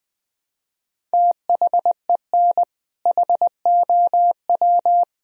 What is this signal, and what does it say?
Morse code, heard as a keyed tone, about 20 wpm, 700 Hz. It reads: THEN HOW
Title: Sentences from Top 1000 Words - Review Part 1 - 20wpm